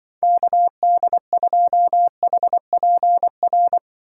Morse code, heard as a keyed tone, about 25 wpm, 700 Hz. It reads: KD2HPR
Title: Speed Racing - USA 2x3 Call Signs 20wpm